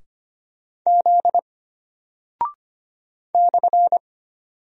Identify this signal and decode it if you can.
Morse code, heard as a keyed tone, about 25 wpm, 700 Hz. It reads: Z /
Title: Single Character (Letter:Number:Punctuation:Prosign) Alt 25wpm